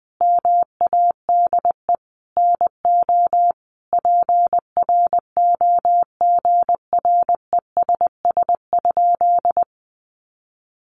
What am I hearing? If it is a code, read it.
Morse code, heard as a keyed tone, about 20 wpm, 700 Hz. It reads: MADE NO PROGRESS?
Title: Sentences from Top 1400 Words 20wpm